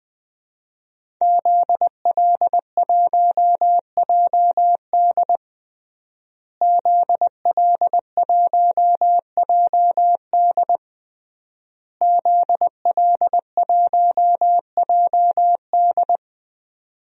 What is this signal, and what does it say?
Morse code, heard as a keyed tone, about 21 wpm, 700 Hz. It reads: ZL1JD ZL1JD ZL1JD
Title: DXCC-100 Call Signs - 5 Characters - Repeated 3x - 20wpm